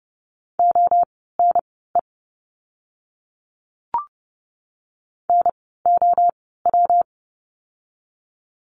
Morse code, heard as a keyed tone, about 29 wpm, spaced as for 17 wpm, 700 Hz. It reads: ONE NOW
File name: course / 30:10wpm / 230 Lesson 37f - Letter x - Words 30wpm